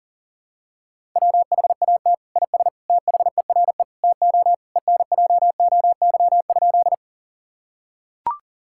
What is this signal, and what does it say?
Morse code, heard as a keyed tone, about 37 wpm, 700 Hz. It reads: WHAT IS THERE TO ENJOY?
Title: Sentences from Top 1000 Words - Review Part 5 - 40wpm